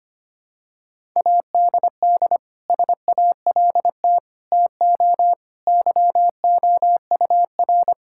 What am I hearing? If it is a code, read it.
Morse code, heard as a keyed tone, about 25 wpm, 700 Hz. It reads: ADD SALT TO YOUR
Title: Sentences from Top 1000 Words - Review Part 5 - 25wpm